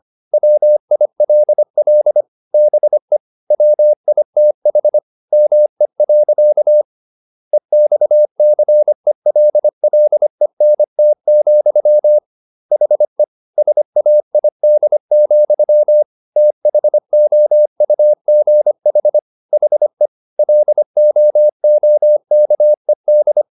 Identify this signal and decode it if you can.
Morse code, heard as a keyed tone, about 25 wpm, 600 Hz. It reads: WILL BE WITH ME. EXCELLENT, HE SAID, THOUGH HE LOOKED